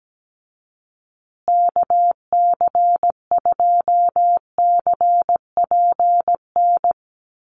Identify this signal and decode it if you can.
Morse code, heard as a keyed tone, about 17 wpm, 700 Hz. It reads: KC2CPN
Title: USA Call Signs Repeated 3x 17wpm